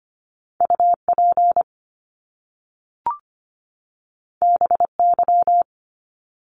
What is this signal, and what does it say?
Morse code, heard as a keyed tone, about 25 wpm, 700 Hz. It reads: UP BY
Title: Top Two Letter Combinations in Words 25wpm